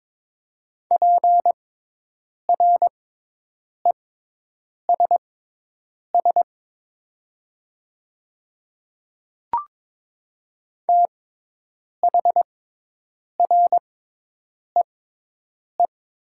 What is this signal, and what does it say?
Morse code, heard as a keyed tone, about 22 wpm, spaced as for 8 wpm, 700 Hz. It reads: PRESS THREE
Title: 6x Character Spacing - Top 300 Words - 22wpm